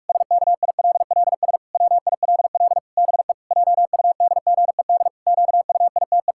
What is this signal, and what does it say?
Morse code, heard as a keyed tone, about 44 wpm, 700 Hz. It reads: SKILLS WILL BE JUDGED QUITE
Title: Pangram Challenge v2 - 45wpm